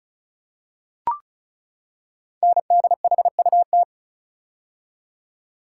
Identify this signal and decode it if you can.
Morse code, heard as a keyed tone, about 36 wpm, 700 Hz. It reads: NDHUT
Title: Code Groups - 5 Letters - 35wpm